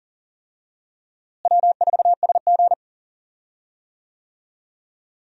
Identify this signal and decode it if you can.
Morse code, heard as a keyed tone, about 40 wpm, 700 Hz. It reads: W4SG
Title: USA Call Signs - 4 Characters - Repeated 2x - 40wpm